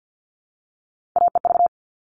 Morse code, heard as a keyed tone, about 50 wpm, 700 Hz. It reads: AEV
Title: All Cut Numbers - 3-Digit 50wpm